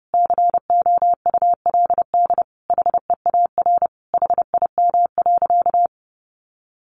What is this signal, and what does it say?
Morse code, heard as a keyed tone, about 30 wpm, 700 Hz. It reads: COULD HEAR HIM.
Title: Sentences from Top 1000 Words 30wpm